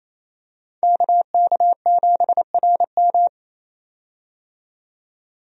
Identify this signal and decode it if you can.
Morse code, heard as a keyed tone, about 28 wpm, 700 Hz. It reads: KK7RM
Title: USA Call Signs - 5 Characters - 28wpm